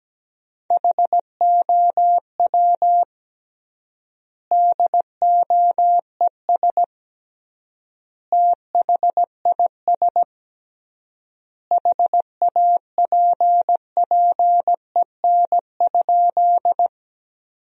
Morse code, heard as a keyed tone, about 17 wpm, 700 Hz. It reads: HOW DOES THIS HAPPEN?
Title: 3x Word-Spacing - Sentences from Top 400 Words - 17wpm